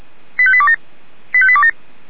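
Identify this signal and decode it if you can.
Five-tone selective call (ZVEI), heard as quick runs of five tones, tones of about 70 ms. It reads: 85827 85827
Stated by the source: ZVEI five-tone selective call received via SDR.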